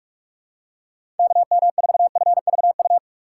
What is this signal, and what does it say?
Morse code, heard as a keyed tone, about 47 wpm, 700 Hz. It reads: KM4FVU
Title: USA Call Signs - 6 Characters - 6x Character Spacing - 45wpm